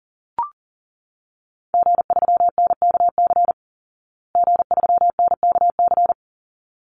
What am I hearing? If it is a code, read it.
Morse code, heard as a keyed tone, about 40 wpm, 700 Hz. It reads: G3NKC G3NKC
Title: Call Signs Repeated 3x - 40wpm